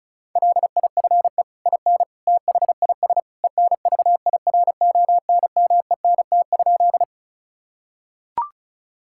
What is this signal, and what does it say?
Morse code, heard as a keyed tone, about 34 wpm, 700 Hz. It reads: LIFE IN THIS ENVIRONMENT?
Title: Sentences from Top 1000 Words - Review Part 5 - 35wpm